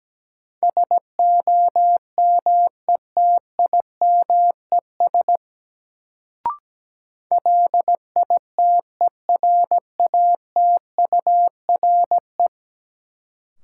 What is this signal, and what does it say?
Morse code, heard as a keyed tone, about 17 wpm, 700 Hz. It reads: SOMETIMES LITERATURE
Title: Sentences from Top 600 Words - New Word Drill - 17wpm